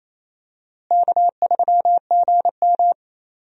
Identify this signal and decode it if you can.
Morse code, heard as a keyed tone, about 28 wpm, 700 Hz. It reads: K3GM
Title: USA Call Signs - 4 Characters - Repeated 3x - 28wpm